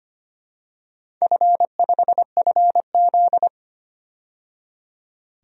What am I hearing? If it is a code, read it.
Morse code, heard as a keyed tone, about 25 wpm, 700 Hz. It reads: F5FZ
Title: DXCC-100 Call Signs - 4 Characters - 25wpm